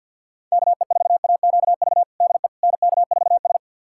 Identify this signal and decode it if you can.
Morse code, heard as a keyed tone, about 51 wpm, 700 Hz. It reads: KE4AQV DE NK4S
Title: USA Callsign DE Callsign 50wpm